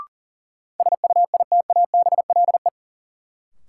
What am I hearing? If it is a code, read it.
Morse code, heard as a keyed tone, about 40 wpm, 700 Hz. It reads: SUITABLE
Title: Sentences from Top 1400 Words - New Word Drill - 40wpm